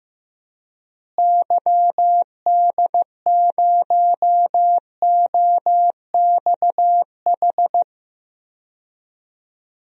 Morse code, heard as a keyed tone, about 15 wpm, 700 Hz. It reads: YD0OXH
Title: DXCC-100 Call Signs - 6 Characters - Repeated 2x - 15wpm